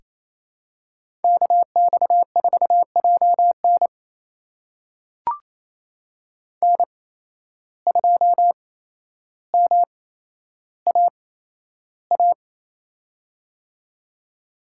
Morse code, heard as a keyed tone, about 28 wpm, spaced as for 12 wpm, 700 Hz. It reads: KX4JN N2MAA
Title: USA Call Signs - 5 Characters - 8x Character Spacing - 28wpm